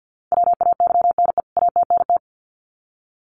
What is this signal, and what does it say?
Morse code, heard as a keyed tone, about 50 wpm, 700 Hz. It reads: WAYNE ATNT
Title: CWT Exchange 50wpm